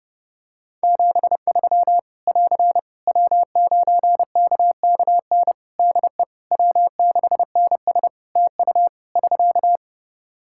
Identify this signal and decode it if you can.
Morse code, heard as a keyed tone, about 32 wpm, 700 Hz. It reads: 73 <AR> W9KKN DE W6NS TU <SK>
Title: Practice QSO - Exchange 4 - Ending - 30wpm